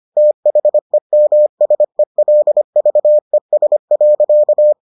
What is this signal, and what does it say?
Morse code, heard as a keyed tone, about 25 wpm, 600 Hz. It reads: THEMSELVES.